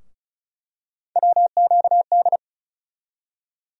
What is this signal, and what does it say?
Morse code, heard as a keyed tone, about 36 wpm, 700 Hz. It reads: WQD
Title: USA Call Sign Postfixes - 3 Letters 35wpm